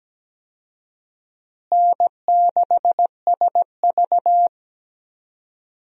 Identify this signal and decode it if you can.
Morse code, heard as a keyed tone, about 17 wpm, 700 Hz. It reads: N6SV
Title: USA Call Signs - 4 Characters - Repeated 3x - 17wpm